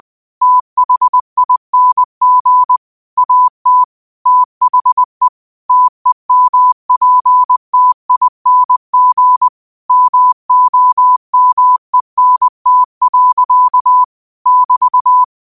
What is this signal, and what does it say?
Morse code, heard as a keyed tone, about 20 wpm, 1000 Hz. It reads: THING AT THE TEMPTING MOMENT. <BT>